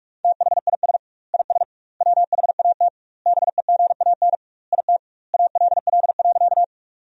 Morse code, heard as a keyed tone, about 43 wpm, 700 Hz. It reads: THIS IS WHAT BEGAN IT ALL.
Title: 2x Word-Spacing - Sentences from Top 300 Words - 45wpm